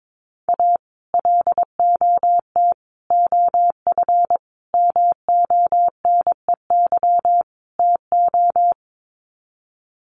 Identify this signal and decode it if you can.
Morse code, heard as a keyed tone, about 22 wpm, 700 Hz. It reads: A LOT OF MONEY TO
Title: Sets of 5 Words Encore2 22wpm